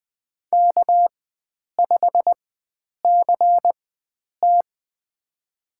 Morse code, heard as a keyed tone, about 20 wpm, spaced as for 9 wpm, 700 Hz. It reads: K5CT
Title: USA Call Signs - 4 Characters - 4x Character Spacing - 20wpm